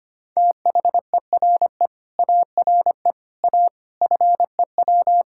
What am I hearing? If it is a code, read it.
Morse code, heard as a keyed tone, about 25 wpm, 700 Hz. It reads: THERE ARE A FEW
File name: practice / 25wpm / Sentences with 3-Word Phrases - 25wpm